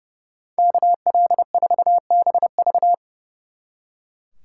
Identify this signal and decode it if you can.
Morse code, heard as a keyed tone, about 38 wpm, 700 Hz. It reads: KL4BV